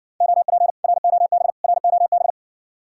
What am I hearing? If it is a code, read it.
Morse code, heard as a keyed tone, about 61 wpm, 700 Hz. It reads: OP ROB ROB